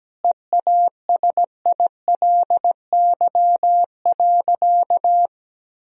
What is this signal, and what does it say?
Morse code, heard as a keyed tone, about 17 wpm, 700 Hz. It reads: EASILY.
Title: Sentences from Top 1000 Words - Review Part 3 - 17wpm